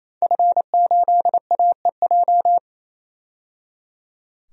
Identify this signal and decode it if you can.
Morse code, heard as a keyed tone, about 28 wpm, 700 Hz. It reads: F8AEJ
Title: DXCC-100 Call Signs - 5 Characters - 28wpm